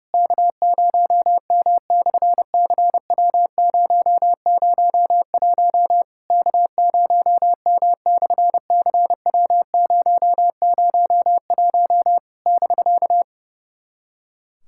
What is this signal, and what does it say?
Morse code, heard as a keyed tone, about 30 wpm, 700 Hz. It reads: K0M/CW001 K0M/CW001 <BK>
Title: SOTA QSO Practice 30wpm